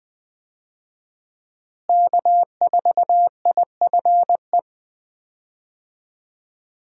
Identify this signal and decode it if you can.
Morse code, heard as a keyed tone, about 20 wpm, 700 Hz. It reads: K4IFE
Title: USA Call Signs - 5 Characters - Repeated 2x - 20wpm